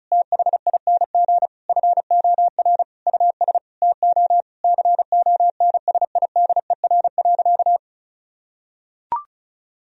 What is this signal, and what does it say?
Morse code, heard as a keyed tone, about 36 wpm, 700 Hz. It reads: THING FOR US TO CONSIDER.
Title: Sentences from Top 1000 Words - Review Part 4 - 35wpm